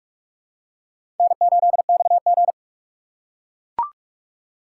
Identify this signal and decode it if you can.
Morse code, heard as a keyed tone, about 41 wpm, 700 Hz. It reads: N8XG
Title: USA Call Signs - 4 Characters - 3x Character Spacing - 45wpm